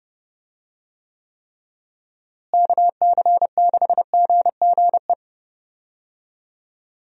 Morse code, heard as a keyed tone, about 30 wpm, 700 Hz. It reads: KC6GGE
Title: USA Call Signs - 6 Characters - Repeated 3x - 30wpm